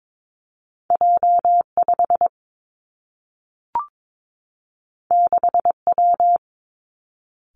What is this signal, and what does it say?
Morse code, heard as a keyed tone, about 22 wpm, 700 Hz. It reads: J 5 6 W
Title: Two Challenging Letter-Number Combinations 22wpm